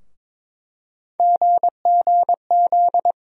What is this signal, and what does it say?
Morse code, heard as a keyed tone, about 22 wpm, 700 Hz. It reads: GGZ
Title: USA Call Sign Postfixes - 3 Letters 22wpm